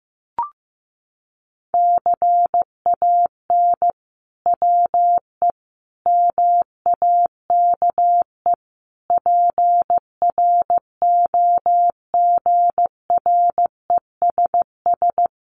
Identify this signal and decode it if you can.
Morse code, heard as a keyed tone, about 15 wpm, 700 Hz. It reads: CAN WE MAKE PROGRESS
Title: Sentences from Top 1500 Words 15wpm